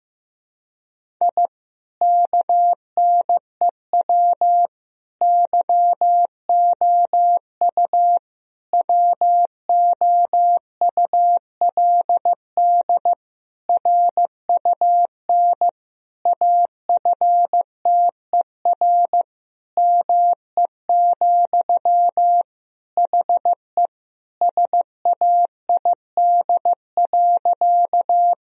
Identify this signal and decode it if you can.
Morse code, heard as a keyed tone, about 15 wpm, 700 Hz. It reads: I KNEW YOU WOULD RUN AFTER ME, HE SAID.
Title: Sentences from Top 1000 Words - Review Part 1 - 15wpm